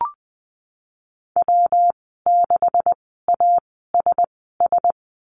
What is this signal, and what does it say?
Morse code, heard as a keyed tone, about 20 wpm, spaced as for 15 wpm, 700 Hz. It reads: W6ASS
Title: 193 Lesson 31g - Letter v - Call Signs 20wpm